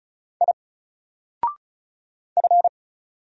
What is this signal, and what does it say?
Morse code, heard as a keyed tone, about 34 wpm, 700 Hz. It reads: I F
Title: Single Letter-Number 35wpm